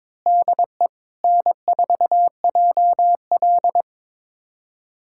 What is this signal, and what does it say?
Morse code, heard as a keyed tone, about 22 wpm, 700 Hz. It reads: DE N4JL
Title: USA Callsign DE Callsign - 4 Characters - 22wpm